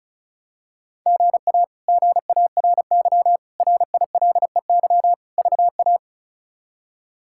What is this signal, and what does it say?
Morse code, heard as a keyed tone, about 36 wpm, 700 Hz. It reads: GA GARY RILEY VA
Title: K1USN SST QSO Practice - 35wpm